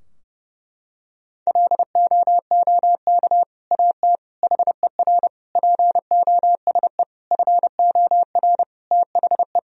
Morse code, heard as a keyed tone, about 30 wpm, 700 Hz. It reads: LOOK AT HER POSE FOR THE
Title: Sentences from Top 1000 Words - Review Part 2 - 30wpm